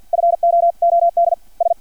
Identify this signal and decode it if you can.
Morse code, heard as a keyed tone, about 50 wpm, 680 Hz. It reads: FOOD S
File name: archived / CRSnet